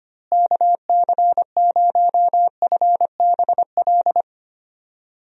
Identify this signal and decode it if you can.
Morse code, heard as a keyed tone, about 25 wpm, 700 Hz. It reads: KC0FBL
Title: USA Call Signs - 6 Characters - Repeated 2x - 25wpm